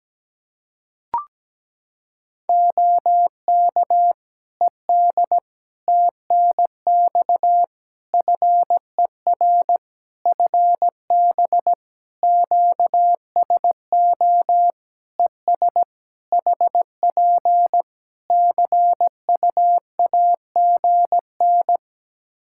Morse code, heard as a keyed tone, about 17 wpm, 700 Hz. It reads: OK ED TNX FER FB QSO ES HP CUAGN
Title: K1USN SST QSO Practice 17wpm